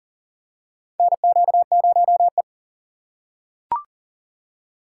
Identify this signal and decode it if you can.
Morse code, heard as a keyed tone, about 40 wpm, 700 Hz. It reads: NQ0E
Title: USA Call Signs - 4 Characters - 3x Character Spacing - 40wpm